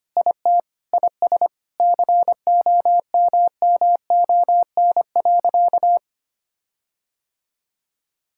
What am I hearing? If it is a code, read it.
Morse code, heard as a keyed tone, about 25 wpm, 700 Hz. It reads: IT IS COMMON.
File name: practice / 25wpm / Sentences with 4-Word Phrases - 25wpm